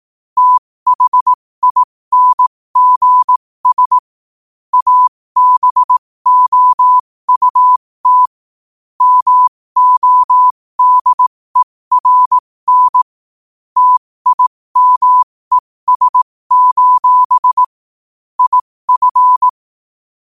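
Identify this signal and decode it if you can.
Morse code, heard as a keyed tone, about 18 wpm, spaced as for 15 wpm, 1000 Hz. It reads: THINGS ABOUT MODERN TIMES: IF
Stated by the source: Morse quote-of-the-day practice at 15 WPM from Morse Resource.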